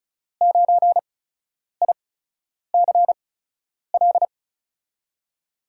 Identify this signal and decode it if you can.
Morse code, heard as a keyed tone, about 36 wpm, spaced as for 10 wpm, 700 Hz. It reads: 9ICL
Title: USA Call Signs - 6 Characters - 8x Character Spacing - 35wpm